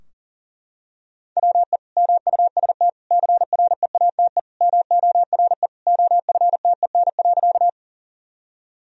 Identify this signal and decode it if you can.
Morse code, heard as a keyed tone, about 40 wpm, 700 Hz. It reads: WE MUST CREATE MORE OFTEN.
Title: Sentences from Top 1000 Words - Review Part 3 - 40wpm